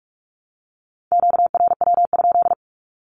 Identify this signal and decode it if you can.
Morse code, heard as a keyed tone, about 44 wpm, 700 Hz. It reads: QRW?
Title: Q-Codes 45wpm